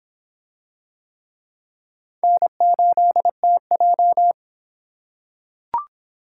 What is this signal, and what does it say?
Morse code, heard as a keyed tone, about 21 wpm, 700 Hz. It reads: N8TJ
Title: Speed Racing - USA 1x2 Call Signs 17wpm